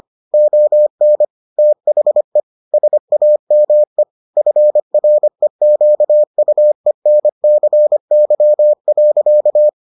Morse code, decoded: ON THE SAME FREQUENCY.

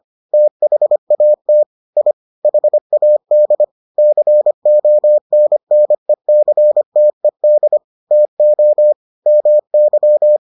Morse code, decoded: THAT I HAD CONNECTED TO MY

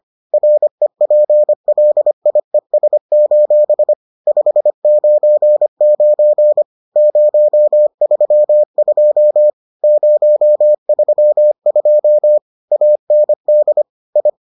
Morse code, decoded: REPLIES: 599 032 032 AND I

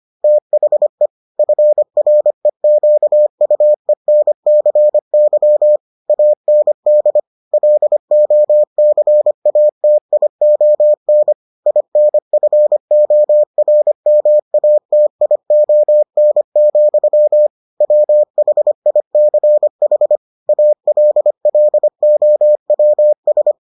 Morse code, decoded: THE FREQUENCY AND LOCATION INFORMATION, WHICH ALLOWS

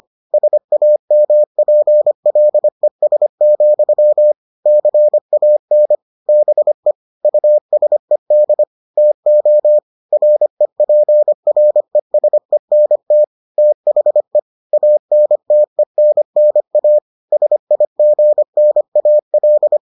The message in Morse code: SAMPLES, CAN BE USED TO REPRESENT THE ANTENNA SIGNAL

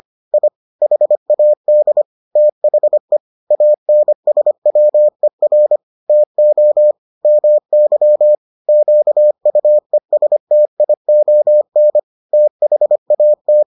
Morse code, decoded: I HAD THE ANSWER TO MY QUESTION THAT

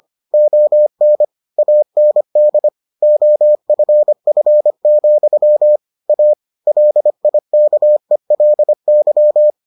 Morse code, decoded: ON AND OFF, A LIKELY

